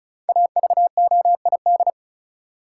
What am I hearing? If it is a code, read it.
Morse code, heard as a keyed tone, about 34 wpm, 700 Hz. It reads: AVOID
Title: Sentences from Top 1000 Words - Review Part 6 - New Word Drill - 35wpm